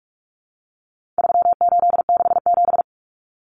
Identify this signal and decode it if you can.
Morse code, heard as a keyed tone, about 44 wpm, 700 Hz. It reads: 3867
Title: Numbers - 4-Digit 45wpm